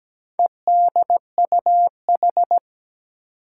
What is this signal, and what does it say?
Morse code, heard as a keyed tone, about 17 wpm, 700 Hz. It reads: EDUH
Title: Four Challenging Letter-Number Combinations 17wpm